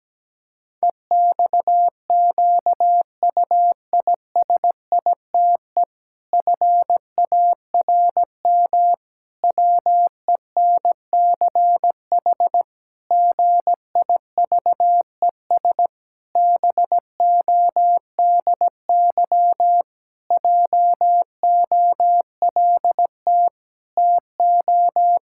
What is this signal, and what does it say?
Morse code, heard as a keyed tone, about 17 wpm, 700 Hz. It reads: EXQUISITE FARM WENCH GIVES BODY JOLT TO
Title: Pangram Challenge v2 - 17wpm